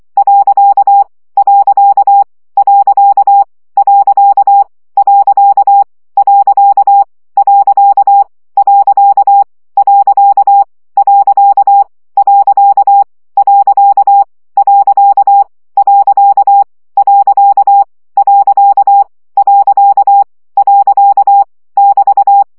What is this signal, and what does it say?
Morse code, heard as a keyed tone, about 24 wpm, 800 Hz. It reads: . . . . . . . . . <BT>
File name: CW gyakorlas pont 20 baud